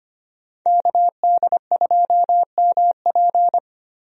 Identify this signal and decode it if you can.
Morse code, heard as a keyed tone, about 25 wpm, 700 Hz. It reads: KD2MP
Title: USA Call Signs - 5 Characters - Repeated 2x - 25wpm